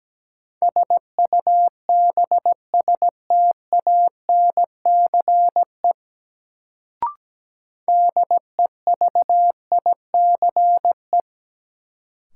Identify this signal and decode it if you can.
Morse code, heard as a keyed tone, about 17 wpm, 700 Hz. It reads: SUBSTANCE DEVICE
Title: Sentences from Top 1500 Words - New Word Drill - 17wpm